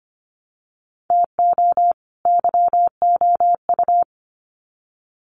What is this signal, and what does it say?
Morse code, heard as a keyed tone, about 25 wpm, 700 Hz. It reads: TO YOU
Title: Sets of 2 Words 25wpm